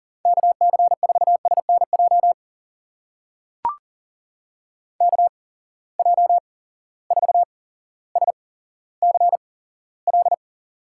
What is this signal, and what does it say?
Morse code, heard as a keyed tone, about 40 wpm, spaced as for 17 wpm, 700 Hz. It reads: KC4SNJ KJ4SCL
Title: USA Call Signs - 6 Characters - 8x Character Spacing - 40wpm